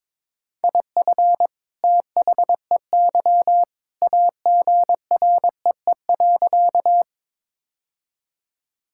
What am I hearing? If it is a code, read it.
Morse code, heard as a keyed tone, about 22 wpm, 700 Hz. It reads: IF THEY AGREE.
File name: practice / 22wpm / Sentences with 2-Word Phrases - 22wpm